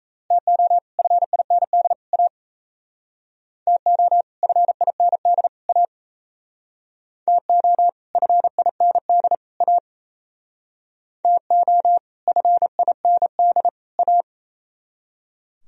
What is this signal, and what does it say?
Morse code, heard as a keyed tone, about 35 wpm, 700 Hz. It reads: TO FIND A TO FIND A TO FIND A TO FIND A
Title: Speed-Racing - Sets of 3 Words 28wpm